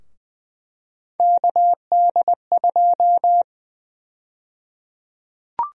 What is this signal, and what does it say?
Morse code, heard as a keyed tone, about 21 wpm, 700 Hz. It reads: KD2
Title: USA Call Sign Prefixes - Repeated 2x - 20wpm